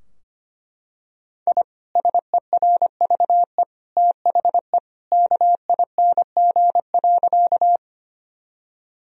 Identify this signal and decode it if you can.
Morse code, heard as a keyed tone, about 25 wpm, 700 Hz. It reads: I SERVE THE KING.
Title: Sentences from Top 1000 Words - Review Part 2 - 25wpm